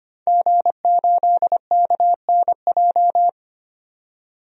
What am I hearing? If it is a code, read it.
Morse code, heard as a keyed tone, about 25 wpm, 700 Hz. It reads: G8KNJ
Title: DXCC-100 Call Signs - 5 Characters - 25wpm